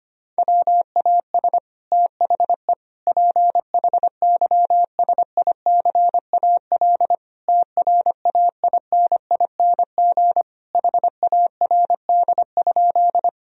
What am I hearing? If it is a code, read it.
Morse code, heard as a keyed tone, about 25 wpm, 700 Hz. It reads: WAS THE PHYSICAL TRAINING HARD?
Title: Sentences from Top 1000 Words - Review Part 4 - 25wpm